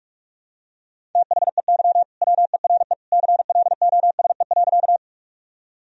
Morse code, heard as a keyed tone, about 45 wpm, 700 Hz. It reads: THEY WERE CLOSE.
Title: Sentences from Top 1000 Words - Review Part 1 - 45wpm